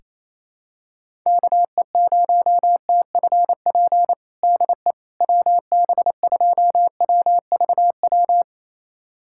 Morse code, heard as a keyed tone, about 28 wpm, 700 Hz. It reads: KE0TFP DE WB2WVW
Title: USA Callsign DE Callsign - 6 Characters - 28wpm